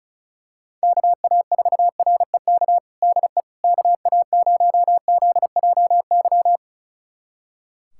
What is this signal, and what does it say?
Morse code, heard as a keyed tone, about 36 wpm, 700 Hz. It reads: KA4REK DE KA0ZJY
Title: USA Callsign DE Callsign 35wpm